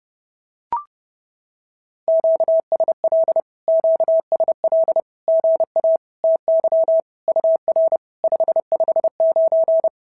Morse code, heard as a keyed tone, about 30 wpm, 650 Hz. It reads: QSL QSL GA TY UR 559
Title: SOTA QSO Practice 30wpm